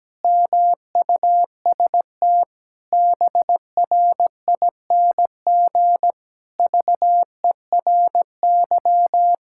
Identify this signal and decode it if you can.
Morse code, heard as a keyed tone, about 17 wpm, 700 Hz. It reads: MUST BRING VERY
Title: Pangram Challenge v2 - 17wpm